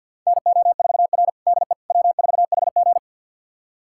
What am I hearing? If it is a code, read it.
Morse code, heard as a keyed tone, about 50 wpm, 700 Hz. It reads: NO4R DE W4HG
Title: USA Callsign DE Callsign - 4 Characters - 50wpm